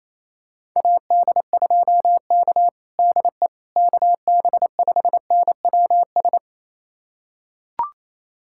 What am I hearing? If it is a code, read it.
Morse code, heard as a keyed tone, about 28 wpm, 700 Hz. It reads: AD2K DE KB5NWS
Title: USA Callsign DE Callsign 28wpm